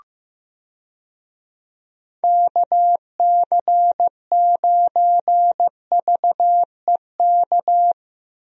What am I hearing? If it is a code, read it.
Morse code, heard as a keyed tone, about 15 wpm, 700 Hz. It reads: KC9VEK
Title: USA Call Signs - 6 Characters - Repeated 3x - 15wpm